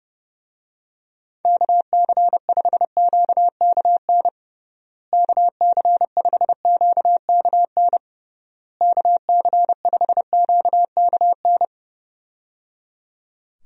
Morse code, heard as a keyed tone, about 30 wpm, 700 Hz. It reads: KC5QKN KC5QKN KC5QKN
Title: USA Call Signs - 6 Characters - Repeated 3x - 30wpm